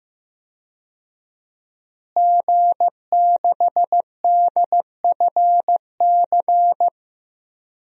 Morse code, heard as a keyed tone, about 15 wpm, 700 Hz. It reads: G6DFC
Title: DXCC-100 Call Signs - 5 Characters - Repeated 3x - 15wpm